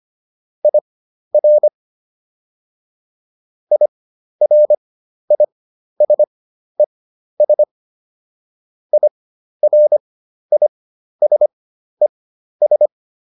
Morse code, heard as a keyed tone, about 25 wpm, spaced as for 12 wpm, 600 Hz. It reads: IR IRISES IRISES